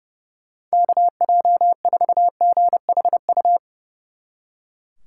Morse code, heard as a keyed tone, about 36 wpm, 700 Hz. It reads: KJ4GHU
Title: Speed Racing - USA 2x3 Call Signs 30wpm